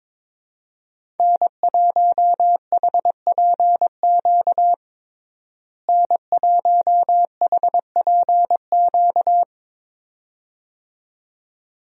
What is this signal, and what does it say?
Morse code, heard as a keyed tone, about 22 wpm, 700 Hz. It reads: N1HPQ N1HPQ
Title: USA Call Signs - 5 Characters - Repeated 2x - 22wpm